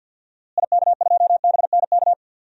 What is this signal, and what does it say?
Morse code, heard as a keyed tone, about 52 wpm, 700 Hz. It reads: IK2BNK
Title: DXCC-100 Call Signs - 6 Characters - Repeated 2x - 50wpm